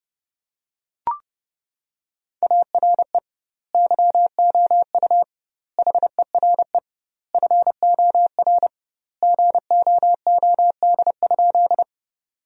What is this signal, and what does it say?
Morse code, heard as a keyed tone, about 30 wpm, 700 Hz. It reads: ARE YOU HERE FOR GOOD?
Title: 2x Word-Spacing - Sentences from Top 200 Words - 30wpm